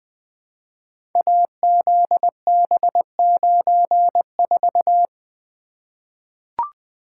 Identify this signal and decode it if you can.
Morse code, heard as a keyed tone, about 20 wpm, 700 Hz. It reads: AZB94
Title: Code Groups - 5 Letter:Numbers - 20wpm